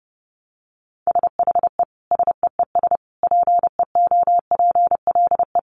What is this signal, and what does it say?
Morse code, heard as a keyed tone, about 30 wpm, 700 Hz. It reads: SHE SEES PEOPLE
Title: Easy Phrases 30wpm